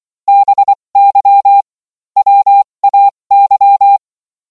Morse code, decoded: BY WAY